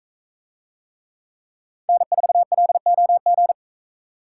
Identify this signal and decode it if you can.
Morse code, heard as a keyed tone, about 40 wpm, 700 Hz. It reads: N4LOG